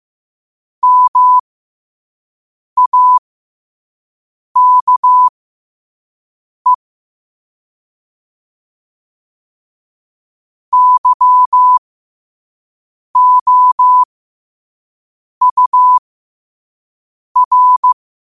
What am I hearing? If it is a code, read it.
Morse code, heard as a keyed tone, about 15 wpm, spaced as for 5 wpm, 1000 Hz. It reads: MAKE YOUR
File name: qotdfiles / 5wpm